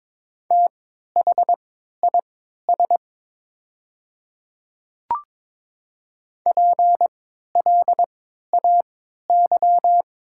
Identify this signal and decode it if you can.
Morse code, heard as a keyed tone, about 22 wpm, spaced as for 13 wpm, 700 Hz. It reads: THIS PLAY
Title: 3x Character Spacing - Top 200 Words - 22wpm